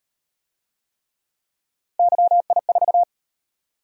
Morse code, heard as a keyed tone, about 38 wpm, 700 Hz. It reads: YI4